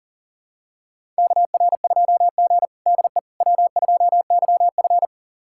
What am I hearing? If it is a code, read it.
Morse code, heard as a keyed tone, about 37 wpm, 700 Hz. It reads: KR2G DE W2YF
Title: USA Callsign DE Callsign - 4 Characters - 40wpm